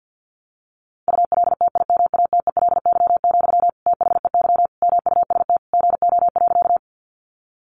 Morse code, heard as a keyed tone, about 50 wpm, 700 Hz. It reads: ULTIMATELY, THEY MUST GO.
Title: Sentences from Top 1200 Words 50wpm